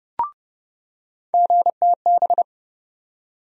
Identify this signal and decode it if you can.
Morse code, heard as a keyed tone, about 30 wpm, 700 Hz. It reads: GTB
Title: USA Call Sign Postfixes - 3 Letters 30wpm